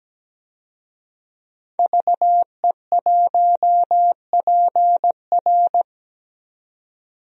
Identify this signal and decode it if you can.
Morse code, heard as a keyed tone, about 17 wpm, 700 Hz. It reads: VE1PR
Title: DXCC-100 Call Signs - 5 Characters - Repeated 3x - 17wpm